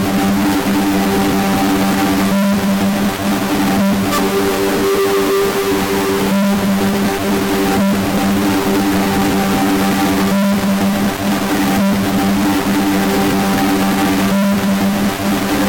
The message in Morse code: none